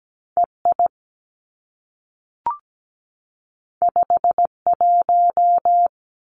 Morse code, heard as a keyed tone, about 17 wpm, 700 Hz. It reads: E I 5 1
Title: Two Challenging Letter-Number Combinations 17wpm